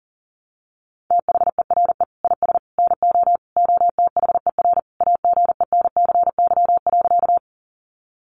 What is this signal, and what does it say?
Morse code, heard as a keyed tone, about 40 wpm, 700 Hz. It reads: THERE IS NO OTHER AGENCY.
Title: Sentences from Top 1300 Words 40wpm